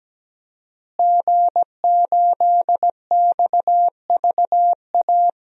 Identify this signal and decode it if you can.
Morse code, heard as a keyed tone, about 17 wpm, 700 Hz. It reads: G8XVA
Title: DXCC-100 Call Signs - 5 Characters - Repeated 2x - 17wpm